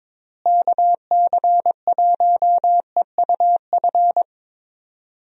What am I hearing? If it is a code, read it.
Morse code, heard as a keyed tone, about 22 wpm, 700 Hz. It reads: KC1EUF
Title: USA Call Signs - 6 Characters - Repeated 2x - 22wpm